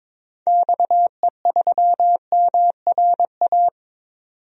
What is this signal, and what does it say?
Morse code, heard as a keyed tone, about 23 wpm, 700 Hz. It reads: XE3MRA
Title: DXCC-100 Call Signs - 6 Characters - 22wpm